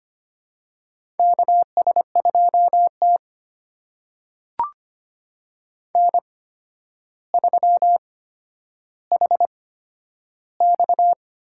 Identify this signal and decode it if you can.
Morse code, heard as a keyed tone, about 25 wpm, spaced as for 11 wpm, 700 Hz. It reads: KS2T N3HX
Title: USA Call Signs - 8x Character Spacing - 25wpm